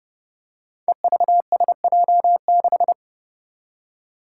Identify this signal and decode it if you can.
Morse code, heard as a keyed tone, about 30 wpm, 700 Hz. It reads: EVSJ6
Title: Five Challenging Letter-Number Combinations 30wpm